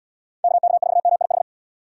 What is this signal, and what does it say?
Morse code, heard as a keyed tone, about 76 wpm, 700 Hz. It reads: KK4MIH